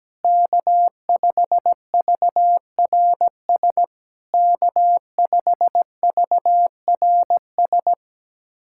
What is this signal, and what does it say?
Morse code, heard as a keyed tone, about 17 wpm, 690 Hz. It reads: K5VRS K5VRS
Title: SOTA QSO Practice 17wpm